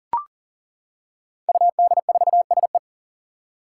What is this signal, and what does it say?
Morse code, heard as a keyed tone, about 40 wpm, 700 Hz. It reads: UD4SE
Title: Five Challenging Letter-Number Combinations 40wpm